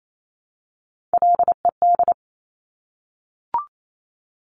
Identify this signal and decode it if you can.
Morse code, heard as a keyed tone, about 28 wpm, 690 Hz. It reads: LED
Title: Top Three Letter Combinations in Words 28wpm